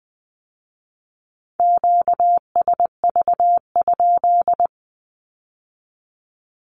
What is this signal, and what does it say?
Morse code, heard as a keyed tone, about 20 wpm, 700 Hz. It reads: QSV?
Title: Q-Codes with Spoken Meaning - 20wpm